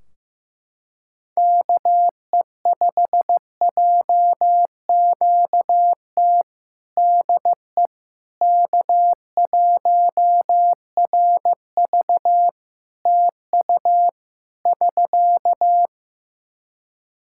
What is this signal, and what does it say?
Morse code, heard as a keyed tone, about 15 wpm, 700 Hz. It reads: KE5JQT DE K1RV TU <SK>
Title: Practice QSO - Exchange 4 - Ending - 15wpm